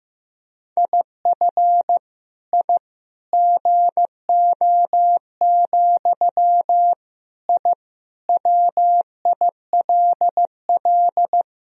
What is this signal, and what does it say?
Morse code, heard as a keyed tone, about 15 wpm, 700 Hz. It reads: IF I GO, I WILL
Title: Sentences from Top 1000 Words - Review Part 3 - 15wpm